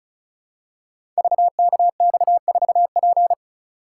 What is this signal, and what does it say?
Morse code, heard as a keyed tone, about 36 wpm, 700 Hz. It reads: VKX4P
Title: Code Groups - 5 Letter:Numbers - 35wpm